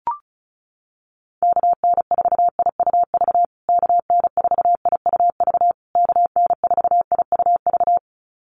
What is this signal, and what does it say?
Morse code, heard as a keyed tone, about 34 wpm, 700 Hz. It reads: KN4IUV KN4IUV KN4IUV
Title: USA Call Signs Repeated 3x 35wpm